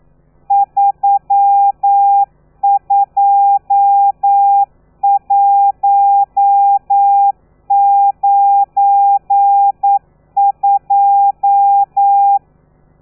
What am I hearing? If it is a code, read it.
Morse code, heard as a keyed tone, about 9 wpm, 800 Hz. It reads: 32192
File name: telegrafia / i2tfj / Lez12 40